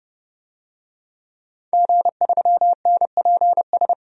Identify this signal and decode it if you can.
Morse code, heard as a keyed tone, about 30 wpm, 700 Hz. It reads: G3NPS
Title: DXCC-100 Call Signs - 5 Characters - 30wpm